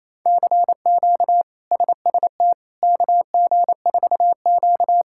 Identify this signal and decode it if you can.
Morse code, heard as a keyed tone, about 27 wpm, 700 Hz. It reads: CQ SST KG4Q
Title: K1USN SST QSO Practice - 28wpm